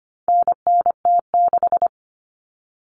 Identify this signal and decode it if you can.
Morse code, heard as a keyed tone, about 25 wpm, 700 Hz. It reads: NNT6
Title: Common Cut Numbers - 4-Digit 25wpm